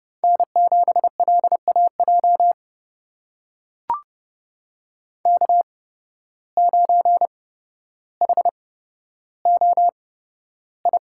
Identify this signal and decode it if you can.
Morse code, heard as a keyed tone, about 30 wpm, spaced as for 13 wpm, 700 Hz. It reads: N7LAJ K9HOI
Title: USA Call Signs - 5 Characters - 8x Character Spacing - 30wpm